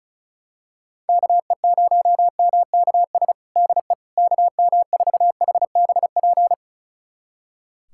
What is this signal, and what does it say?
Morse code, heard as a keyed tone, about 34 wpm, 700 Hz. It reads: KE0MKS DE KM4HBP
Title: USA Callsign DE Callsign - 6 Characters - 35wpm